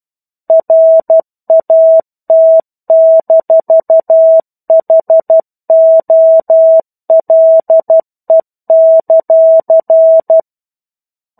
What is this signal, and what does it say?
Morse code, heard as a keyed tone, about 12 wpm, 650 Hz. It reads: RAT-HOLE;
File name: 01-Chapter-01 12 WPM